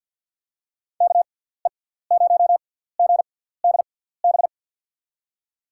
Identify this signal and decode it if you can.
Morse code, heard as a keyed tone, about 50 wpm, spaced as for 17 wpm, 700 Hz. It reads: KE0GDB
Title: USA Call Signs - 6 Characters - 6x Character Spacing - 50wpm